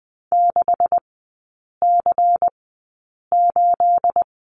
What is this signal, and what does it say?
Morse code, heard as a keyed tone, about 20 wpm, spaced as for 8 wpm, 690 Hz. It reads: 6C8
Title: 212 Lesson 34e - Number 8 - Three Characters 20wpm